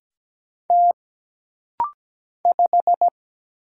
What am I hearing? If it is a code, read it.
Morse code, heard as a keyed tone, about 17 wpm, 700 Hz. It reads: T 5
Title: Single Letter-Number Alt - Rapid-Fire 17wpm